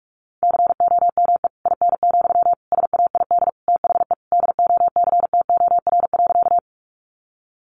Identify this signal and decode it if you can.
Morse code, heard as a keyed tone, about 44 wpm, 700 Hz. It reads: COME IN, SAID THE DOCTOR.
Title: Sentences from Top 1500 Words 45wpm